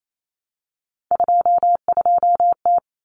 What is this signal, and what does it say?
Morse code, heard as a keyed tone, about 28 wpm, 700 Hz. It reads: 22T
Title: Common Cut Numbers - 3-Digit 28wpm